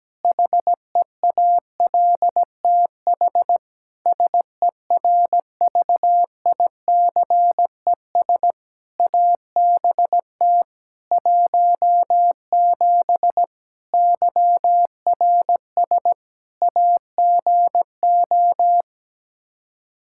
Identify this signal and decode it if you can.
Morse code, heard as a keyed tone, about 17 wpm, 700 Hz. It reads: HEALTH SERVICES ABT 17 YRS AGO
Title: K1USN SST QSO Practice 17wpm